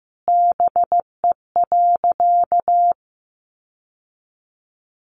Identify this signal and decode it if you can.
Morse code, heard as a keyed tone, about 15 wpm, 700 Hz. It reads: BE.
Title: Sentences from Top 1100 Words 15wpm